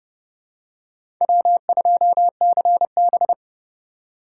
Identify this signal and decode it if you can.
Morse code, heard as a keyed tone, about 32 wpm, 700 Hz. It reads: W2CB
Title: USA Call Signs - 4 Characters - 6x Character Spacing - 30wpm